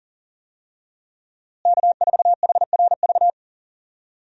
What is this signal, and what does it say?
Morse code, heard as a keyed tone, about 39 wpm, 700 Hz. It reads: K4HRV
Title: USA Call Signs - 5 Characters - Repeated 2x - 40wpm